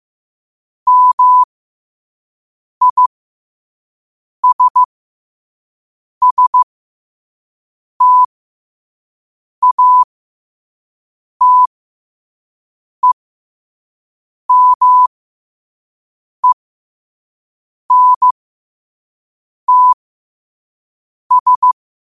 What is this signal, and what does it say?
Morse code, heard as a keyed tone, about 15 wpm, spaced as for 5 wpm, 1000 Hz. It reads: MISSTATEMENTS
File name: qotdfiles / 5wpm